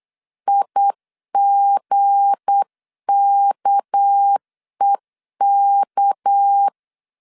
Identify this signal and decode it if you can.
Morse code, heard as a keyed tone, about 8 wpm, 800 Hz. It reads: IGKEK
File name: LEZIONI / 7 GF9Z / 7 G 8wpm 3118